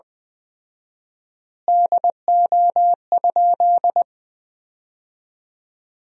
Morse code, heard as a keyed tone, about 20 wpm, 700 Hz. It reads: DO?
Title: 4x Word-Spacing - Sentences from Top 100 Words - 20wpm